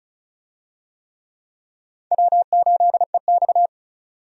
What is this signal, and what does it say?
Morse code, heard as a keyed tone, about 36 wpm, 700 Hz. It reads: W8EX
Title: USA Call Signs - 4 Characters - 35wpm